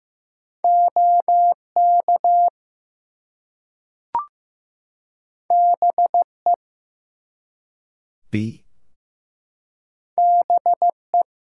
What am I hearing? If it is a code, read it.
Morse code, heard as a keyed tone, about 15 wpm, 700 Hz. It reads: OK BE BE
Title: Two Letter Words - 15wpm